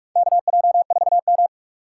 Morse code, heard as a keyed tone, about 46 wpm, 700 Hz. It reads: KJ4M
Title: USA Call Signs - 2x Character Spacing - 45wpm